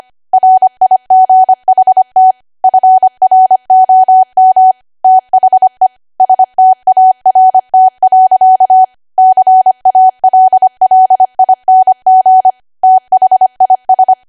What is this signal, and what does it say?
Morse code, heard as a keyed tone, about 25 wpm, 750 Hz. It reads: RIGHT FROM THE START. CALLING THIS
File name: cwa / W1AW-Level3 / 25WPM